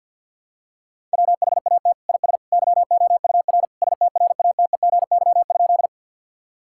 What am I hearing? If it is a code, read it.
Morse code, heard as a keyed tone, about 49 wpm, 700 Hz. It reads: WHAT IS YOUR STRATEGY?
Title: Sentences from Top 1000 Words - Review Part 5 - 50wpm